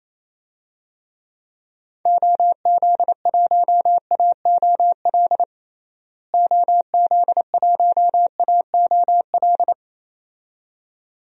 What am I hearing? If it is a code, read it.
Morse code, heard as a keyed tone, about 28 wpm, 700 Hz. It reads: OZ1AOL OZ1AOL
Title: DXCC-100 Call Signs - 6 Characters - Repeated 2x - 28wpm